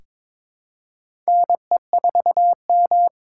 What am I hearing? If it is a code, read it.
Morse code, heard as a keyed tone, about 22 wpm, 700 Hz. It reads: NE4M
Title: K1USN SST QSO Practice - 22wpm